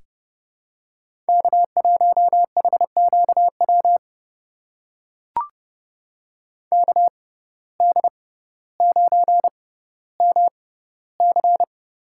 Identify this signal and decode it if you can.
Morse code, heard as a keyed tone, about 32 wpm, spaced as for 16 wpm, 700 Hz. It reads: K1HQW KD9MC
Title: USA Call Signs - 5 Characters - 6x Character Spacing - 30wpm